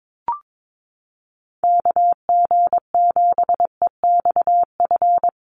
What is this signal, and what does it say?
Morse code, heard as a keyed tone, about 22 wpm, 700 Hz. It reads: KG7EXF
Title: USA Call Signs Repeated 3x 22wpm